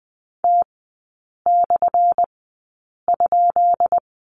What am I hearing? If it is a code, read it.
Morse code, heard as a keyed tone, about 20 wpm, spaced as for 8 wpm, 690 Hz. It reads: T/?
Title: 212 Lesson 34e - Number 8 - Three Characters 20wpm